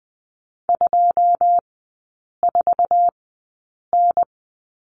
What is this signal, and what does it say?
Morse code, heard as a keyed tone, about 20 wpm, spaced as for 8 wpm, 700 Hz. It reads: 24N
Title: 087 Lesson 15e - Number 5 - Three Characters 20wpm